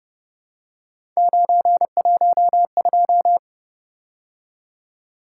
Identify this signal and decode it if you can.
Morse code, heard as a keyed tone, about 30 wpm, 700 Hz. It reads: 912
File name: Speed-Racing - Three Letter-Number Combinations - 25wpm